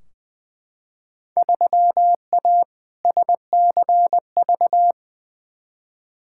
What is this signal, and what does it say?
Morse code, heard as a keyed tone, about 20 wpm, 700 Hz. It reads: 3A SCV
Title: Field Day Exchanges - 20wpm